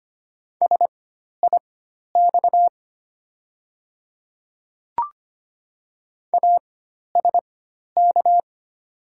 Morse code, heard as a keyed tone, about 25 wpm, spaced as for 12 wpm, 700 Hz. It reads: SIX ASK
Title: 4x Character Spacing - Top 500 Words - 25wpm